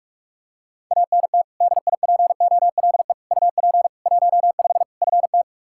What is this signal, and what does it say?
Morse code, heard as a keyed tone, about 46 wpm, 700 Hz. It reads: ANT DIPOLE UP 15 FT
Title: Practice QSO - Exchange 2 - Info - 45wpm